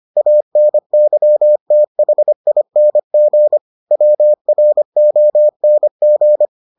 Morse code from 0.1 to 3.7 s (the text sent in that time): ANYTHING